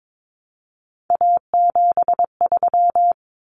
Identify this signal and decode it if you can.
Morse code, heard as a keyed tone, about 22 wpm, 700 Hz. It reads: A73
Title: Common Cut Numbers - 3-Digit 22wpm